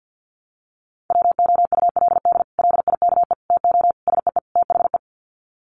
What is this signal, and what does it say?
Morse code, heard as a keyed tone, about 50 wpm, 700 Hz. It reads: WOULD LIKE TO SEE THE
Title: Sets of 5 Words Encore2 50wpm